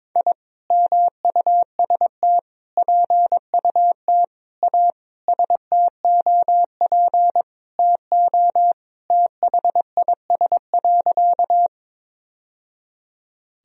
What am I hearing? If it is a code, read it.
Morse code, heard as a keyed tone, about 22 wpm, 700 Hz. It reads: I MUST PUT A STOP TO THIS.
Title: Sentences from Top 1000 Words - Review Part 1 - 22wpm